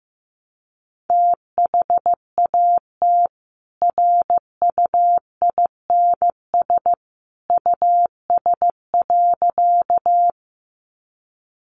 Morse code, heard as a keyed tone, about 15 wpm, 700 Hz. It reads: THAT RUINS US.
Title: A Sound Of Thunder 15wpm